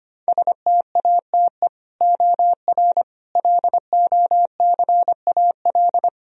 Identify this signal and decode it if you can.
Morse code, heard as a keyed tone, about 25 wpm, 700 Hz. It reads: STATE OR LOCAL
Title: Sentences from Top 1000 Words - Review Part 3 - 25wpm